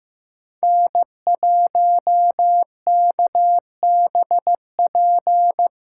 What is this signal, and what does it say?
Morse code, heard as a keyed tone, about 15 wpm, 700 Hz. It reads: N1KBP
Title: USA Call Signs - 6x Character Spacing - 15wpm